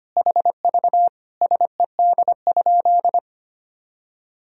Morse code, heard as a keyed tone, about 25 wpm, 700 Hz. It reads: HV SED?
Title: Sentences from Top 100 Words and Abbreviations 25wpm